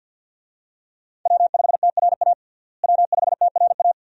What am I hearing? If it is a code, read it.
Morse code, heard as a keyed tone, about 48 wpm, 700 Hz. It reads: W5TRA W5TRA
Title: USA Call Signs - 5 Characters - Repeated 2x - 50wpm